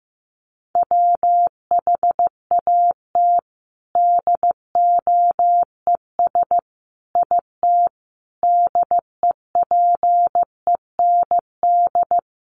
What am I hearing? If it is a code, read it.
Morse code, heard as a keyed tone, about 15 wpm, 700 Hz. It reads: WHAT DOES IT DEPEND
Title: Sentences from Top 1300 Words 15wpm